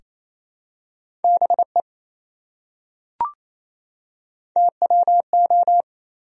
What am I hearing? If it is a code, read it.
Morse code, heard as a keyed tone, about 27 wpm, 700 Hz. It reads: BE TWO
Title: Top 100 Words 28wpm